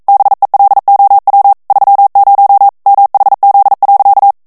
Morse code, heard as a keyed tone, about 42 wpm, 800 Hz. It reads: BELOW 30 MHZ.
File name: 40 mots minute